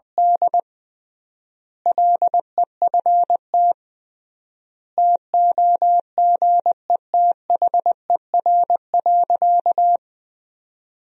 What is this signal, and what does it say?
Morse code, heard as a keyed tone, about 20 wpm, 700 Hz. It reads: D LEFT TOGETHER.
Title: 3x Word-Spacing - Sentences from Top 400 Words - 20wpm